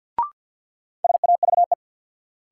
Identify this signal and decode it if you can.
Morse code, heard as a keyed tone, about 50 wpm, 700 Hz. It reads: SAVE
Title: Sentences from Top 1100 Words - New Word Drill Alt - 50wpm